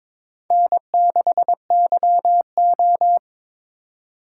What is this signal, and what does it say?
Morse code, heard as a keyed tone, about 22 wpm, 700 Hz. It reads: N6YO
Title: USA Call Signs - 4 Characters - 2x Character Spacing - 22wpm